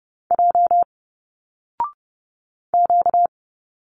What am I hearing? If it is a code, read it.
Morse code, heard as a keyed tone, about 30 wpm, 700 Hz. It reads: J Q
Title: Single Letters 30wpm